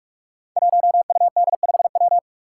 Morse code, heard as a keyed tone, about 44 wpm, 700 Hz. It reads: 1UD5W
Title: Five Challenging Letter-Number Combinations 45wpm